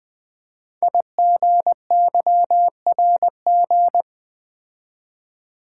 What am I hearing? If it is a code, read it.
Morse code, heard as a keyed tone, about 20 wpm, 700 Hz. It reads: IGYRG
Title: Code Groups - 5 Letters - 20wpm